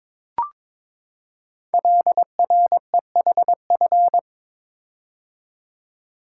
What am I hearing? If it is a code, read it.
Morse code, heard as a keyed tone, about 22 wpm, 700 Hz. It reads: LREHF
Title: Code Groups - 5 Letters - 22wpm